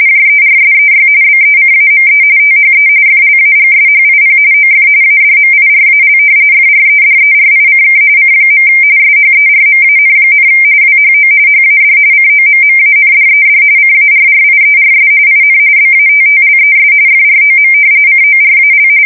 RTTY, about 45.45 baud. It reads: JERSEY BECAME THE FLAGSHIP OF ADMIRAL FREDERICK C. SHERMAN, WHO HAD COMMANDED CARRIERS IN BATTLES FROM CORAL
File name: fsk170 / battleshipnj